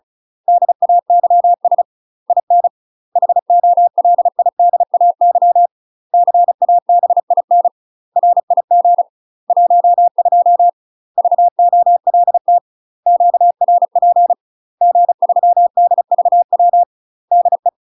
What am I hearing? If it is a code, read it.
Morse code, heard as a keyed tone, about 36 wpm, 700 Hz. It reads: DAYS IN HOLIDAY CABIN RIG 12 VOLT QRP G3DVW DE
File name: UK MASTER QSOs TRAINING / 35WPM / UK MASTER QSOs 0022